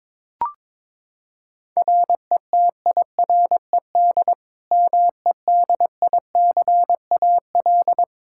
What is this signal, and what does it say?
Morse code, heard as a keyed tone, about 22 wpm, 700 Hz. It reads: RETIRED MEDICAL
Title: Practice QSO - Exchange 3 - More Info - 22wpm